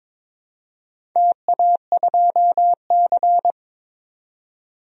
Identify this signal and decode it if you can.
Morse code, heard as a keyed tone, about 22 wpm, 700 Hz. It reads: TA2C
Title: DXCC-100 Call Signs - 4 Characters - 22wpm